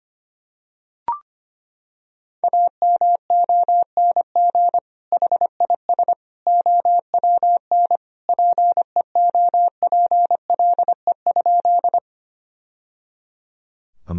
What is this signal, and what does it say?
Morse code, heard as a keyed tone, about 25 wpm, 700 Hz. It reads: AMONG HIS OWN PEOPLE?
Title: Sentences from Top 1000 Words - Review Part 3 - 25wpm